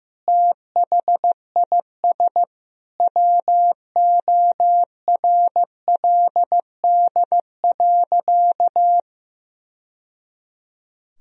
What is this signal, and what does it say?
Morse code, heard as a keyed tone, about 15 wpm, 700 Hz. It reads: THIS WORLD.
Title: Sentences from Top 1000 Words - Review Part 1 - 15wpm